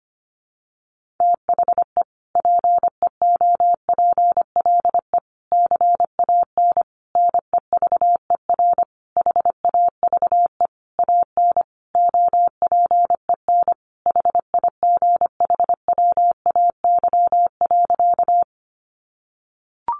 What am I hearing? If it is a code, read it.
Morse code, heard as a keyed tone, about 25 wpm, 700 Hz. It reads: THE PEOPLE CAN NEVER HAVE AN OPEN HIGHWAY.
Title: Sentences from Top 1400 Words 25wpm